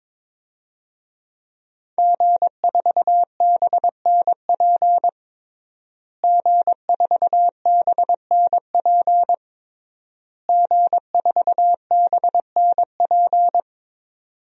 Morse code, heard as a keyed tone, about 22 wpm, 700 Hz. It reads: G4BNP G4BNP G4BNP
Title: DXCC-100 Call Signs - 5 Characters - Repeated 3x - 22wpm